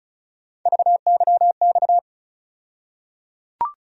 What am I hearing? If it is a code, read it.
Morse code, heard as a keyed tone, about 36 wpm, 700 Hz. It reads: VYX
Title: USA Call Sign Postfixes - 3 Letters 35wpm